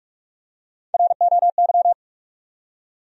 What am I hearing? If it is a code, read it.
Morse code, heard as a keyed tone, about 46 wpm, 700 Hz. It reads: ROY
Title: USA Common Names - Max 3 Letters - 45wpm